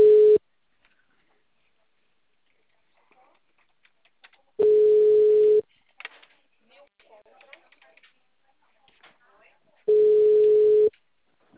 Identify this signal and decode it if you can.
Ringback tone (Europe, 425 Hz), with speech.